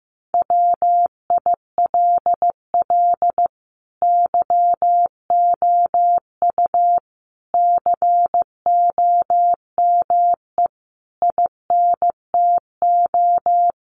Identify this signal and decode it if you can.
Morse code, heard as a keyed tone, about 15 wpm, 700 Hz. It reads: WILL YOU COME INTO
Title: Sentences from Top 700 Words 15wpm